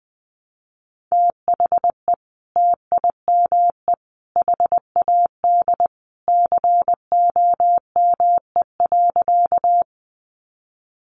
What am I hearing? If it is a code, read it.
Morse code, heard as a keyed tone, about 20 wpm, 700 Hz. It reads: THE TIME HAD COME.
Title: Sentences from Top 100 Words 20wpm